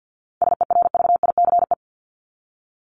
Morse code, heard as a keyed tone, about 50 wpm, 700 Hz. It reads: SERVICE
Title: Top 501-750 Words 50wpm